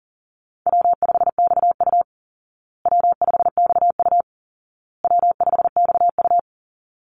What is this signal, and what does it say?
Morse code, heard as a keyed tone, about 40 wpm, 700 Hz. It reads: W5XU W5XU W5XU
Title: Call Signs Repeated 2x - 40wpm